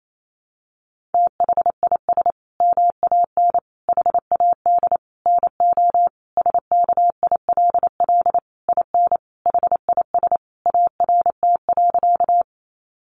Morse code, heard as a keyed tone, about 28 wpm, 700 Hz. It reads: THIS MAN HAD NO SKILL IN HIS ART.
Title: Sentences from Top 1200 Words 28wpm